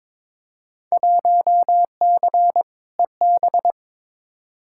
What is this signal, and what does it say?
Morse code, heard as a keyed tone, about 22 wpm, 700 Hz. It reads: 1C EB
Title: Field Day Exchanges Encore - 22wpm